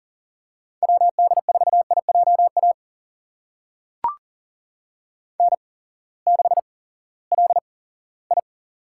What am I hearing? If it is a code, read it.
Morse code, heard as a keyed tone, about 40 wpm, spaced as for 20 wpm, 700 Hz. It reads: WD4IJA N6LI
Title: USA Call Signs - 8x Character Spacing - 40wpm